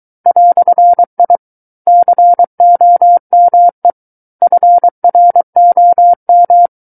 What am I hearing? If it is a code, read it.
Morse code, heard as a keyed tone, about 23 wpm, 700 Hz. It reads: "I COME FROM
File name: Book One / 14-Chapter 14 - 23 WPM